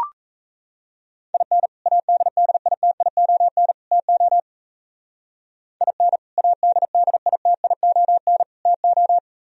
Morse code, heard as a keyed tone, about 33 wpm, 700 Hz. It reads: IN ADDITION TO IN ADDITION TO
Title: Speed-Racing - Sets of 3 Words 28wpm